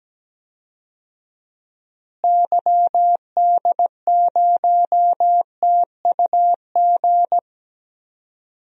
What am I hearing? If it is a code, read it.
Morse code, heard as a keyed tone, about 17 wpm, 700 Hz. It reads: YD0TUG
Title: DXCC-100 Call Signs - 6 Characters - Repeated 2x - 17wpm